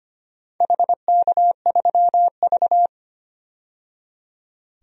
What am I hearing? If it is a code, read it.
Morse code, heard as a keyed tone, about 25 wpm, 700 Hz. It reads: HK3V
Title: DXCC-100 Call Signs - 4 Characters - 25wpm